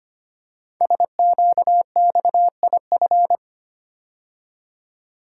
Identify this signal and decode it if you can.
Morse code, heard as a keyed tone, about 25 wpm, 700 Hz. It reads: SQXIF
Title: Code Groups - 5 Letter:Numbers - 25wpm